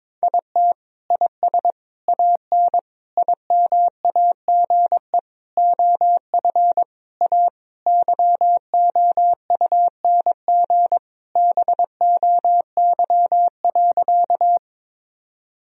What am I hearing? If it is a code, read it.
Morse code, heard as a keyed tone, about 22 wpm, 700 Hz. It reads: IT IS AN IMAGE OF A YOUNG BOY.
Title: Sentences from Top 1000 Words - Review Part 4 - 22wpm